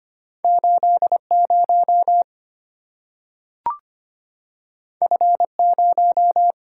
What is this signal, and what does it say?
Morse code, heard as a keyed tone, about 25 wpm, 690 Hz. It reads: 80 F0
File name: Two Letter-Number Combinations - 25wpm